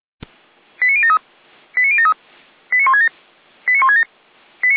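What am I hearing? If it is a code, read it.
Five-tone selective call (ZVEI), heard as quick runs of five tones, tones of about 70 ms. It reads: 89073 89073 89167 89167
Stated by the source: ZVEI five-tone selective call received by SDR.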